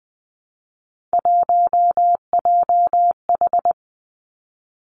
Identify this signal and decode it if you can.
Morse code, heard as a keyed tone, about 20 wpm, 700 Hz. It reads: 1JH
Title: Three Challenging Letter-Number Combinations 20wpm